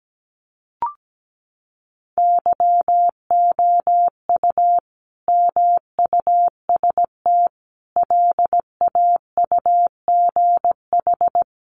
Morse code, heard as a keyed tone, about 17 wpm, 700 Hz. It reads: YOU MUST LAUGH
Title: Sentences from Top 600 Words 17wpm